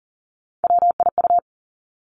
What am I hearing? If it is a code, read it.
Morse code, heard as a keyed tone, about 42 wpm, 700 Hz. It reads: WIU
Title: Three Challenging Letter-Number Combinations 40wpm